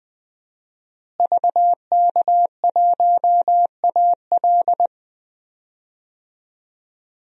Morse code, heard as a keyed tone, about 20 wpm, 700 Hz. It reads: VK1AL
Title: DXCC-100 Call Signs - 5 Characters - Repeated 3x - 20wpm